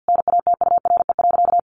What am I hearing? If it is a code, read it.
Morse code, heard as a keyed tone, about 50 wpm, 700 Hz. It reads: NATURE.
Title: Sentences from Top 700 Words 50wpm